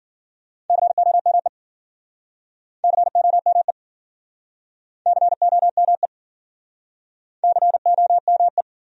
Morse code, heard as a keyed tone, about 53 wpm, 700 Hz. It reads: COME COME COME COME